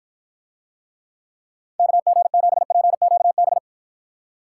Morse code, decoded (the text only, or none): KG7PQB